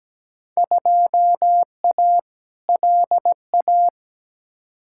Morse code, heard as a keyed tone, about 17 wpm, 700 Hz. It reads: 2A LA
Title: Field Day Exchanges Encore - 17wpm